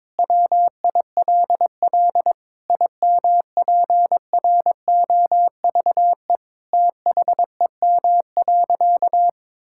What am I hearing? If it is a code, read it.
Morse code, heard as a keyed tone, about 22 wpm, 700 Hz. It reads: WILL IMPROVE THEM.
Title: Sentences from Top 1000 Words - Review Part 5 - 22wpm